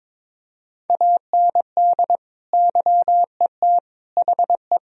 Morse code, heard as a keyed tone, about 22 wpm, 700 Hz. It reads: AND YET HE
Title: Sentences from Top 1000 Words - Review Part 3 - 22wpm